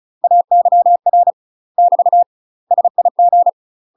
Morse code, heard as a keyed tone, about 34 wpm, 700 Hz. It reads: AYR <BT> SIG
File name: UK MASTER QSOs TRAINING / 35WPM / UK MASTER QSOs 0037